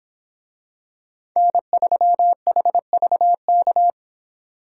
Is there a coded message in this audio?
Yes: Morse code from 1.4 to 3.9 s, heard as a keyed tone, about 25 wpm, 700 Hz. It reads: N3HVK